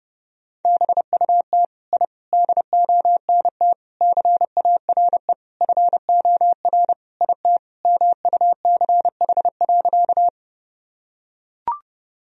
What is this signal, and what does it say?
Morse code, heard as a keyed tone, about 32 wpm, 700 Hz. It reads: BUT I DONT CARE FOR IT MUCH.
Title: Sentences from Top 1000 Words - Review Part 1 - 30wpm